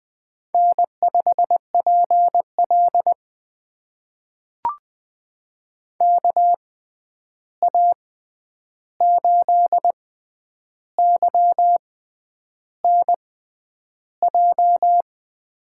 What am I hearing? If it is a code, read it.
Morse code, heard as a keyed tone, about 20 wpm, spaced as for 9 wpm, 700 Hz. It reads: N5PL KA8YNJ
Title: USA Call Signs - 6x Character Spacing - 20wpm